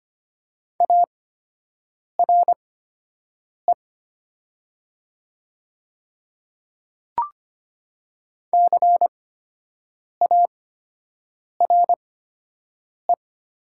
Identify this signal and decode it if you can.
Morse code, heard as a keyed tone, about 25 wpm, spaced as for 7 wpm, 700 Hz. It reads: ARE CARE
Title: 8x Character Spacing - Top 500 Words - 25wpm